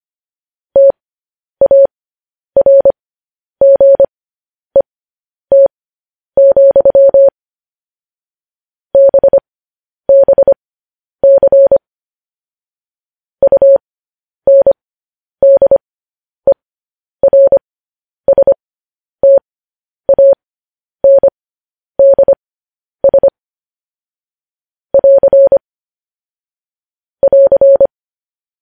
Morse code, heard as a keyed tone, about 25 wpm, spaced as for 10 wpm, 550 Hz. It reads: TARGET, BBC UNDERSTANDS <AR> <AR>
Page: news-headlines-in-morse-code-at-10-wpm